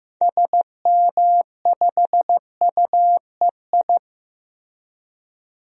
Morse code, heard as a keyed tone, about 15 wpm, 700 Hz. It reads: SM5UEI
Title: DXCC-100 Call Signs - 6 Characters - Repeated 2x - 15wpm